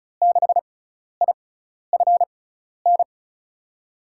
Morse code, heard as a keyed tone, about 34 wpm, spaced as for 12 wpm, 700 Hz. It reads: 6IFN
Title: USA Call Signs - 6x Character Spacing - 35wpm